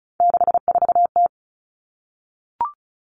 Morse code, heard as a keyed tone, about 34 wpm, 700 Hz. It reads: 64T
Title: Common Cut Numbers - 3-Digit 35wpm